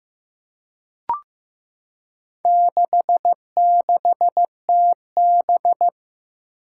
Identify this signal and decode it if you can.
Morse code, heard as a keyed tone, about 15 wpm, 700 Hz. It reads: 66TB
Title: Four Challenging Letter-Number Combinations 15wpm